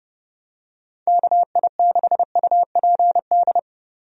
Morse code, heard as a keyed tone, about 32 wpm, 700 Hz. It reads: KI6UPD
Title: USA Call Signs - 6 Characters - Repeated 3x - 30wpm